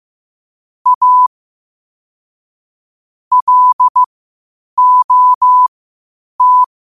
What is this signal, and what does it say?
Morse code, heard as a keyed tone, about 15 wpm, spaced as for 8 wpm, 1000 Hz. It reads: A LOT
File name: qotdfiles / 8wpm